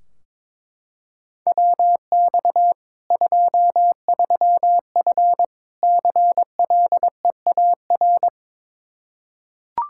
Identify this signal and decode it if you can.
Morse code, heard as a keyed tone, about 22 wpm, 700 Hz. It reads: WX 23F CLEAR
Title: K1USN SST QSO Practice 22wpm